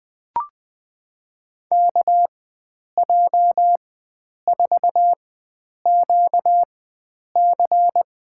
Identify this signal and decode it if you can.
Morse code, heard as a keyed tone, about 21 wpm, spaced as for 9 wpm, 700 Hz. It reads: KJ4QC
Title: USA Call Signs - 4x Character Spacing - 20wpm